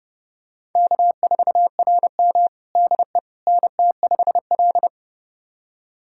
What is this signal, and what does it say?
Morse code, heard as a keyed tone, about 30 wpm, 700 Hz. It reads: K4RM DE NT5L
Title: USA Callsign DE Callsign - 4 Characters - 30wpm